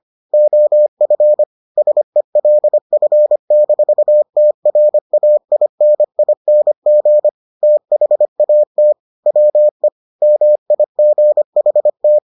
Morse code, heard as a keyed tone, about 25 wpm, 600 Hz. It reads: OF SELF-TRAINING THAT WE MIGHT